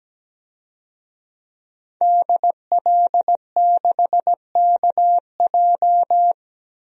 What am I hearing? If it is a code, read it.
Morse code, heard as a keyed tone, about 17 wpm, 700 Hz. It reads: DL6KJ
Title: DXCC-100 Call Signs - 5 Characters - 17wpm